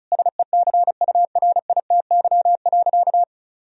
Morse code, heard as a keyed tone, about 36 wpm, 700 Hz. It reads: SECURITY.
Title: Sentences from Top 1000 Words - Review Part 4 - 35wpm